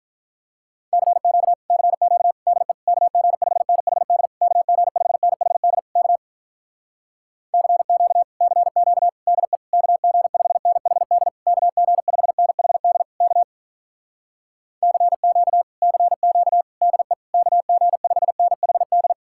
Morse code, heard as a keyed tone, about 47 wpm, 700 Hz. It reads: CQ CQ DE KG5NHD KG5NHD K CQ CQ DE KG5NHD KG5NHD K CQ CQ DE KG5NHD